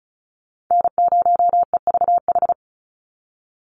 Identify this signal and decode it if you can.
Morse code, heard as a keyed tone, about 34 wpm, 700 Hz. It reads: N0EVH
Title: NA SOTA Chaser Callsigns 35wpm